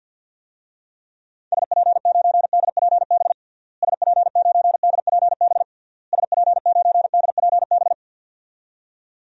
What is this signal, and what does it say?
Morse code, heard as a keyed tone, about 52 wpm, 700 Hz. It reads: SP9DPB SP9DPB SP9DPB
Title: DXCC-100 Call Signs - 6 Characters - Repeated 3x - 50wpm